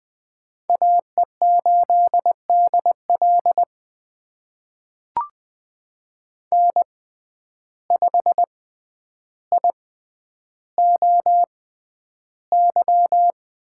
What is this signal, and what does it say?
Morse code, heard as a keyed tone, about 20 wpm, spaced as for 10 wpm, 700 Hz. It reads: AE8DL N5IOY
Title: USA Call Signs - 5 Characters - 6x Character Spacing - 20wpm